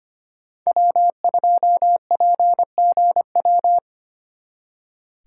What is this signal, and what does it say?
Morse code, heard as a keyed tone, about 30 wpm, 700 Hz. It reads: W2PGW
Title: Speed Racing - USA 1x3 Call Signs 25wpm